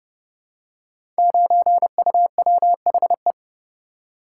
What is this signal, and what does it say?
Morse code, heard as a keyed tone, about 29 wpm, 700 Hz. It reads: 9UWHE
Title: Code Groups - 5 Letter:Numbers - 30wpm